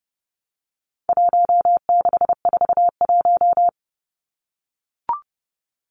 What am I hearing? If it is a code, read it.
Morse code, heard as a keyed tone, about 30 wpm, 700 Hz. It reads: 1641
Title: Numbers - Contest Style - 4-Digit 30wpm